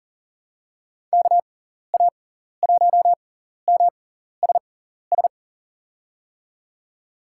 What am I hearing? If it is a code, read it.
Morse code, heard as a keyed tone, about 40 wpm, spaced as for 14 wpm, 700 Hz. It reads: KA1MSS
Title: USA Call Signs - 6 Characters - 6x Character Spacing - 40wpm